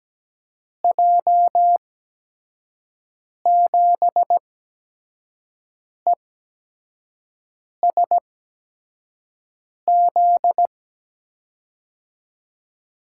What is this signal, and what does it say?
Morse code, heard as a keyed tone, about 17 wpm, spaced as for 5 wpm, 700 Hz. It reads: J7ESZ
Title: USA Call Signs - 6 Characters - 8x Character Spacing - 17wpm